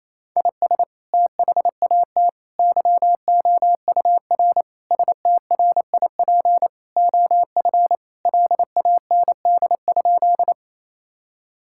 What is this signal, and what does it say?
Morse code, heard as a keyed tone, about 28 wpm, 700 Hz. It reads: IS THAT YOUR STRIP OF LAND?
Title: Sentences from Top 1600 Words 28wpm